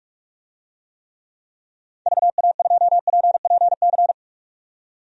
Morse code, heard as a keyed tone, about 41 wpm, 700 Hz. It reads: VA2PPC
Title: DXCC-100 Call Signs - 6 Characters - 45wpm